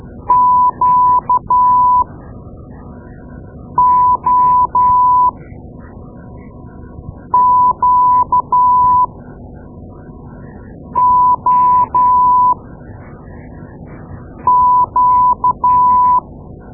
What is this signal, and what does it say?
Morse code, heard as a keyed tone, about 13 wpm, spaced as for 4 wpm, 1000 Hz. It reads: QOQOQ
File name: Morse Lessons / Morse Lessons / Tape-2A